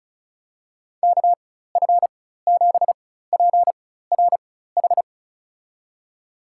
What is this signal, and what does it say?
Morse code, heard as a keyed tone, about 36 wpm, spaced as for 16 wpm, 700 Hz. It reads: KF7PRH
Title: USA Call Signs - 6 Characters - 4x Character Spacing - 35wpm